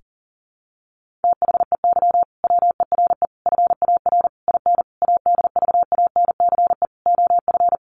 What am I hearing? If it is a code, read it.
Morse code, heard as a keyed tone, about 40 wpm, 700 Hz. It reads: THEY WERE FAR IN ADVANCE OF
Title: Sentences from Top 1400 Words 40wpm